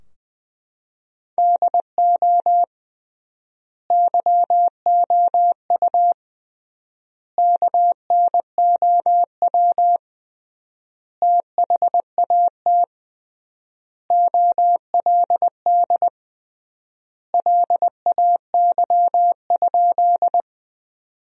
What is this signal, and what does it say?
Morse code, heard as a keyed tone, about 20 wpm, 700 Hz. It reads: DO YOU KNOW THAT OLD LAY?
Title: 3x Word-Spacing - Sentences from Top 400 Words - 20wpm